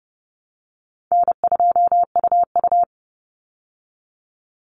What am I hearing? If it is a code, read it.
Morse code, heard as a keyed tone, about 30 wpm, 700 Hz. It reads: N2UU
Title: Easy Call Signs Repeated 2x - 30wpm